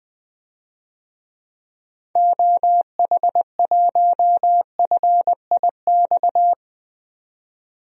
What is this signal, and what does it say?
Morse code, heard as a keyed tone, about 20 wpm, 700 Hz. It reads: OH1FIX
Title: DXCC-100 Call Signs - 6 Characters - 20wpm